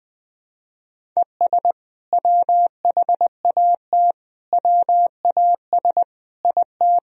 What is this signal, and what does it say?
Morse code, heard as a keyed tone, about 21 wpm, 700 Hz. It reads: ES WHAT WAS IT
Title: Sentences from Top 100 Words and Abbreviations 20wpm